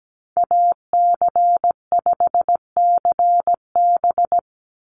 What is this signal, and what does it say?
Morse code, heard as a keyed tone, about 17 wpm, 700 Hz. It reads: AC5CB
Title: USA Call Signs 17wpm